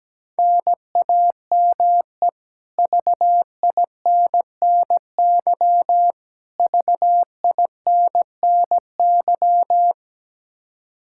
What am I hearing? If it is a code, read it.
Morse code, heard as a keyed tone, about 17 wpm, 700 Hz. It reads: NAME VINNY VINNY
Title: K1USN SST QSO Practice 17wpm